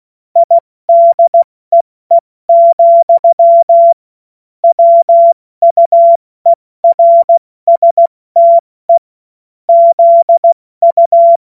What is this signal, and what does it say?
Morse code, heard as a keyed tone, about 16 wpm, 680 Hz. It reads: IDEE, WUERSTE ZU